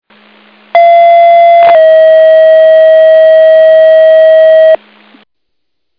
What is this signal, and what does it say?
Two-tone page, heard as a one-second tone, then a three-second tone, about 680 then 640 Hz. It is heard from 0.7 to 4.8 s.